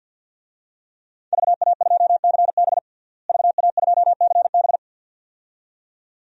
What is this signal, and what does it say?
Morse code, heard as a keyed tone, about 53 wpm, 700 Hz. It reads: VA2CB VA2CB
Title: DXCC-100 Call Signs - 5 Characters - Repeated 2x - 50wpm